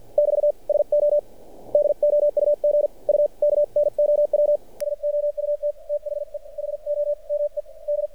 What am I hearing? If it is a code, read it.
Morse code, heard as a keyed tone, about 50 wpm, spaced as for 44 wpm, 590 Hz. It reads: <BT> SO DOUG U KNOW SOUT THE HOME D
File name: archived / CRSnet